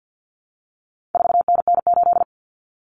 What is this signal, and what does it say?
Morse code, heard as a keyed tone, about 50 wpm, 700 Hz. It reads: 4NN8
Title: Common Cut Numbers - 4-Digit 50wpm